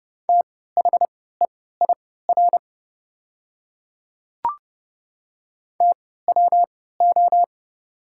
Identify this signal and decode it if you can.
Morse code, heard as a keyed tone, about 31 wpm, spaced as for 17 wpm, 700 Hz. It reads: THEIR TWO
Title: 3x Character Spacing - Top 100 Words - 30wpm